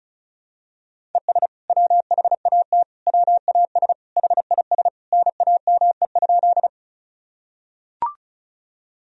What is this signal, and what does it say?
Morse code, heard as a keyed tone, about 36 wpm, 700 Hz. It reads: ES WHAT WAS HIS NAME?
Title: Sentences from Top 200 Words and Abbreviations 35wpm